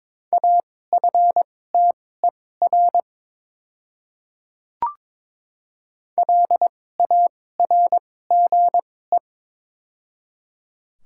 Morse code, heard as a keyed tone, about 22 wpm, spaced as for 16 wpm, 700 Hz. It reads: AFTER LARGE
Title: 2x Character Spacing - Top 200 Words - 22wpm